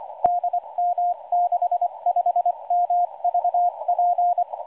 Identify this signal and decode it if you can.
Morse code, heard as a keyed tone, about 25 wpm, 710 Hz, with radio noise. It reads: DM65MVP